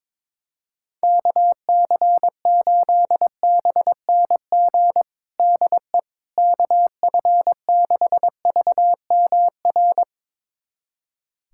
Morse code, heard as a keyed tone, about 22 wpm, 700 Hz. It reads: KC8BNG DE KF6VMR
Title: USA Callsign DE Callsign - 6 Characters - 22wpm